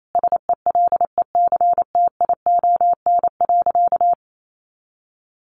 Morse code, heard as a keyed tone, about 28 wpm, 700 Hz. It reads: SELECTION.
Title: Sentences from Top 1400 Words 28wpm